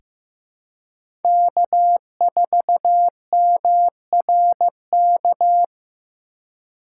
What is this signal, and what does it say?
Morse code, heard as a keyed tone, about 15 wpm, 700 Hz. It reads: K4MRK
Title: USA Call Signs - 5 Characters - 4x Character Spacing - 15wpm